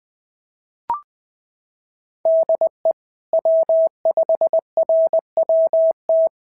Morse code, heard as a keyed tone, about 17 wpm, 650 Hz. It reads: DE W5RWT
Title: SOTA QSO Practice 20wpm